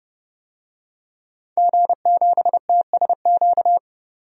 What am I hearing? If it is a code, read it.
Morse code, heard as a keyed tone, about 30 wpm, 700 Hz. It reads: G7TSQ
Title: DXCC-100 Call Signs - 5 Characters - 30wpm